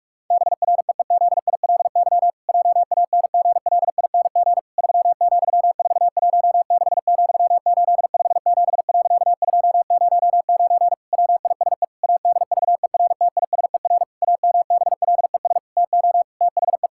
Morse code, heard as a keyed tone, about 42 wpm, 700 Hz. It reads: BREEZILY JANGLING 3,416,857.209 WISE ADVERTISER AMBLES TO THE
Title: Pangram Challenge v2 - 45wpm